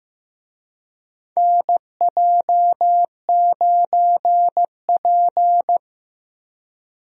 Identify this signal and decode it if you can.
Morse code, heard as a keyed tone, about 15 wpm, 700 Hz. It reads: NJ9P
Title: USA Call Signs - 4 Characters - 15wpm